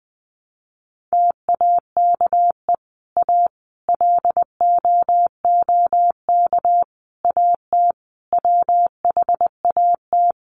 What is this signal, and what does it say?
Morse code, heard as a keyed tone, about 20 wpm, 700 Hz. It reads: TAKE A LOOK AT WHAT
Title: Sets of 5 Words Encore2 20wpm